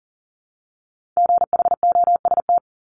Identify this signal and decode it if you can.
Morse code, heard as a keyed tone, about 40 wpm, 700 Hz. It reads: GHOST
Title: Multi-letter Phonemes Repeated 3x 40wpm